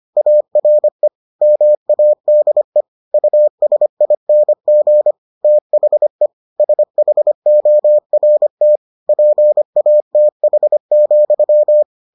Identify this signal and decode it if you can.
Morse code, heard as a keyed tone, about 25 wpm, 600 Hz. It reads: ARE MADE USING THE SHORT PATH,